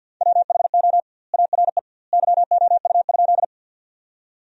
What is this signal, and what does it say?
Morse code, heard as a keyed tone, about 50 wpm, 700 Hz. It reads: WHO ARE YOU?
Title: 2x Word-Spacing - Sentences from Top 100 Words - 50wpm